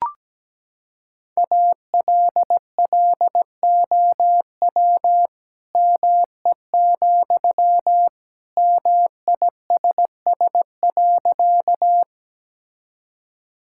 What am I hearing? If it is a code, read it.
Morse code, heard as a keyed tone, about 17 wpm, 700 Hz. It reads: ALLOW ME, MISS.
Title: Sentences from Top 1000 Words - Review Part 5 - 17wpm